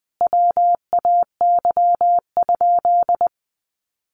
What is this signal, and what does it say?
Morse code, heard as a keyed tone, about 20 wpm, 700 Hz. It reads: WAY?
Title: Sentences from Top 1200 Words 20wpm